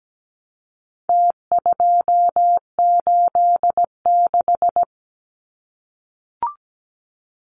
Morse code, heard as a keyed tone, about 17 wpm, 700 Hz. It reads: T286
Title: Numbers - Contest Style - 4-Digit 17wpm